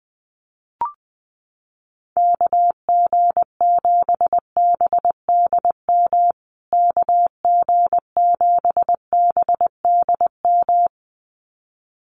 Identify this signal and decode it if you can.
Morse code, heard as a keyed tone, about 19 wpm, 700 Hz. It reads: KG7BDM KG7BDM
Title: USA Call Signs Repeated 2x 20wpm